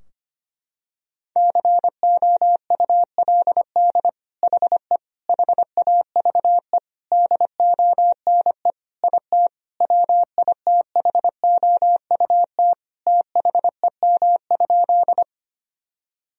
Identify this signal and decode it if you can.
Morse code, heard as a keyed tone, about 25 wpm, 700 Hz. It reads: COULD HE HAVE DONE IT WITHOUT THEM?
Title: Sentences from Top 1000 Words - Review Part 3 - 25wpm